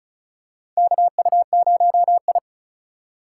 Morse code, heard as a keyed tone, about 34 wpm, 700 Hz. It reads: KU0I
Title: USA Call Signs - 4 Characters - 4x Character Spacing - 35wpm